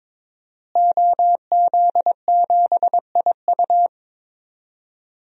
Morse code, heard as a keyed tone, about 22 wpm, 700 Hz. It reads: OZ7IU
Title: DXCC-100 Call Signs - 5 Characters - Repeated 2x - 22wpm